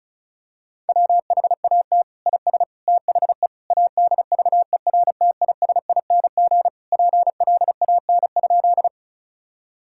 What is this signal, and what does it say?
Morse code, heard as a keyed tone, about 36 wpm, 700 Hz. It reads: WHAT IS THE ADVERTISING PLAN?
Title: Sentences from Top 1000 Words - Review Part 6 - 35wpm